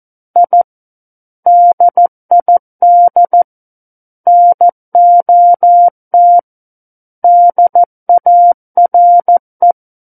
Morse code, heard as a keyed tone, about 14 wpm, 700 Hz. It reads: I DID NOT DARE
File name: Book One / 05-Chapter 5 - 14 WPM